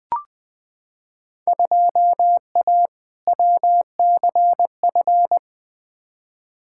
Morse code, heard as a keyed tone, about 20 wpm, 700 Hz. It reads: 2A WCF
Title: Field Day Exchanges Encore - 20wpm